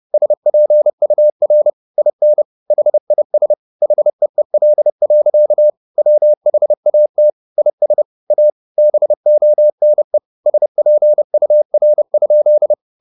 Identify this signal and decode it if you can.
Morse code, heard as a keyed tone, about 30 wpm, 600 Hz. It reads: SPUR IN HIS HEEL. WHAT IS A BONE SPUR?
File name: the old man and the sea / the old man and the sea 1 30wpm